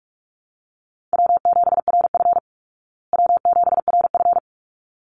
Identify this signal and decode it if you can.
Morse code, heard as a keyed tone, about 44 wpm, 700 Hz. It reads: W7RF W7RF
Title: Easy Call Signs Repeated 2x - 45wpm